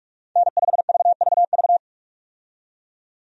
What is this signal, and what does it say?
Morse code, heard as a keyed tone, about 42 wpm, 700 Hz. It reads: N5VVV
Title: USA Call Signs - 5 Characters - 45wpm